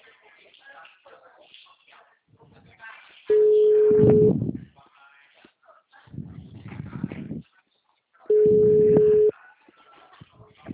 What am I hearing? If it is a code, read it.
Ringback tone (Europe, 425 Hz), with speech.